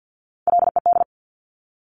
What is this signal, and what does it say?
Morse code, heard as a keyed tone, about 50 wpm, 700 Hz. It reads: LED
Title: Top Three Letter Combinations in Words 50wpm